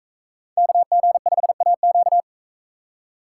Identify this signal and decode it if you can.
Morse code, heard as a keyed tone, about 41 wpm, 700 Hz. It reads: KG5AQ